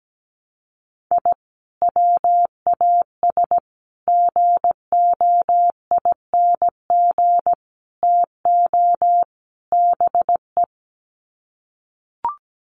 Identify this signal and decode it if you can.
Morse code, heard as a keyed tone, about 17 wpm, 700 Hz. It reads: I WAS GOING TO BE
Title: Sets of 5 Words 17wpm